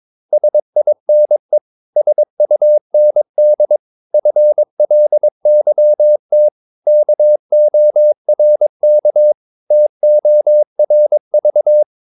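Morse code, heard as a keyed tone, about 22 wpm, 600 Hz. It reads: SINE SUND FLYT KORK TORV
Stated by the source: Morse practice of short Danish words at 22 wpm, from CW med Gnister.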